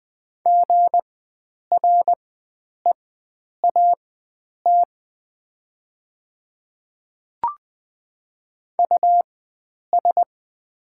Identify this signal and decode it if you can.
Morse code, heard as a keyed tone, about 20 wpm, spaced as for 9 wpm, 690 Hz. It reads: GREAT US
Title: 4x Character Spacing - Top 300 Words - 20wpm